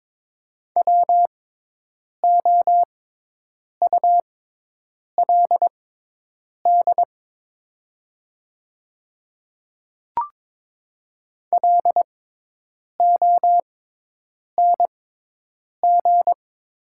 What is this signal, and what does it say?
Morse code, heard as a keyed tone, about 22 wpm, spaced as for 8 wpm, 700 Hz. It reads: WOULD LONG
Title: 6x Character Spacing - Top 100 Words - 22wpm